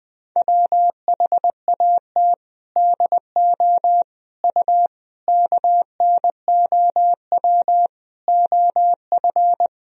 Morse code, heard as a keyed tone, about 20 wpm, 700 Hz. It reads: WHAT DO U KNOW OF
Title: Sentences from Top 100 Words and Abbreviations 20wpm